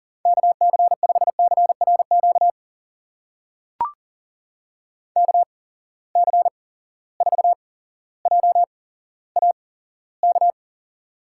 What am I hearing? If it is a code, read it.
Morse code, heard as a keyed tone, about 40 wpm, spaced as for 17 wpm, 700 Hz. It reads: KC5CRQ KC4JAK
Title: USA Call Signs - 6 Characters - 8x Character Spacing - 40wpm